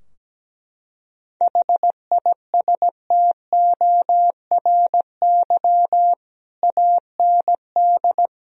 Morse code, heard as a keyed tone, about 17 wpm, 700 Hz. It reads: HISTORY AND
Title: Sentences from Top 1000 Words - Review Part 2 - 17wpm